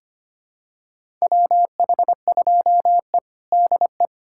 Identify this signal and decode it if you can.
Morse code, heard as a keyed tone, about 25 wpm, 700 Hz. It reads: WH2E DE
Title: USA Callsign DE Callsign - 4 Characters - 25wpm